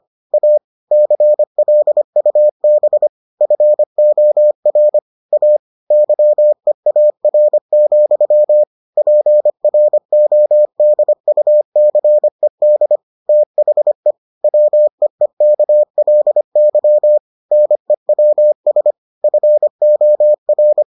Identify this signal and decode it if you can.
Morse code, heard as a keyed tone, about 25 wpm, 600 Hz. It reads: A CLUB FOR A YEAR, PRODUCED THE WEEKLY NEWS FOR